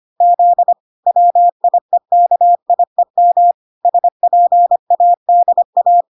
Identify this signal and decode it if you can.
Morse code, heard as a keyed tone, about 25 wpm, 700 Hz. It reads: Z WIEKIEM SPADA